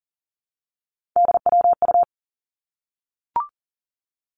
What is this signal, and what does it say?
Morse code, heard as a keyed tone, about 38 wpm, 700 Hz. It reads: DWU
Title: Three Challenging Letter-Number Combinations 40wpm